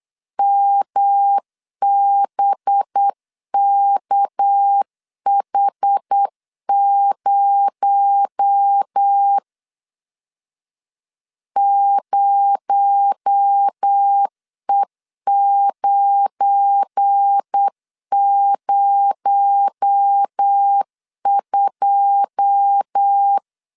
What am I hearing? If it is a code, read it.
Morse code, heard as a keyed tone, about 8 wpm, 800 Hz. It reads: MBKH0 0E902
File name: CW / DET / 1 9 1500C S A 8wpm 5637